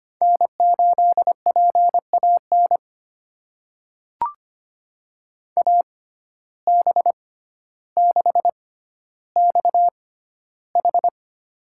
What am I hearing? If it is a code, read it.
Morse code, heard as a keyed tone, about 25 wpm, spaced as for 13 wpm, 700 Hz. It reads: N8PAN AB6XH
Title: USA Call Signs - 5 Characters - 6x Character Spacing - 25wpm